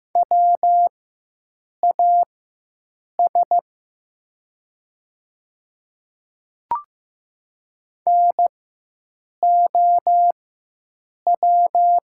Morse code, heard as a keyed tone, about 15 wpm, spaced as for 7 wpm, 700 Hz. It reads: WAS NOW
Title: 4x Character Spacing - Top 100 Words - 15wpm